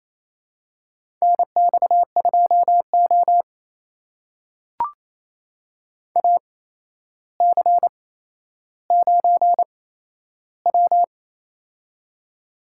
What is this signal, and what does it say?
Morse code, heard as a keyed tone, about 28 wpm, spaced as for 12 wpm, 700 Hz. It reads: NX2O AC9W
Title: USA Call Signs - 4 Characters - 8x Character Spacing - 28wpm